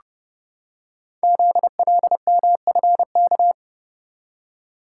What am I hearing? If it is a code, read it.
Morse code, heard as a keyed tone, about 30 wpm, 700 Hz. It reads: ZLMFK
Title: Code Groups - 5 Letters - 30wpm